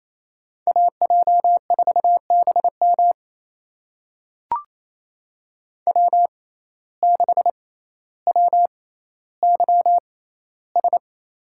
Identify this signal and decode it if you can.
Morse code, heard as a keyed tone, about 28 wpm, spaced as for 14 wpm, 700 Hz. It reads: AJ4BM W6WYS
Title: USA Call Signs - 5 Characters - 6x Character Spacing - 28wpm